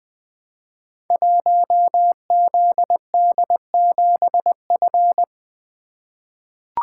Morse code, heard as a keyed tone, about 20 wpm, 700 Hz. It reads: 1ZD7F
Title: Code Groups - 5 Letter:Numbers - 20wpm